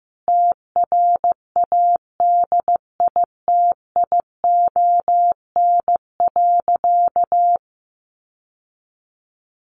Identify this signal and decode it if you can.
Morse code, heard as a keyed tone, about 15 wpm, 700 Hz. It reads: TRADITION.
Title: Sentences from Top 1500 Words 15wpm